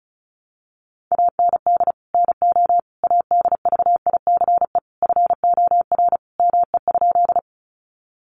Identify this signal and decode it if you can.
Morse code, heard as a keyed tone, about 34 wpm, 700 Hz. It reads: AND NO ADVICE FOR ME?
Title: Sentences from Top 1100 Words 35wpm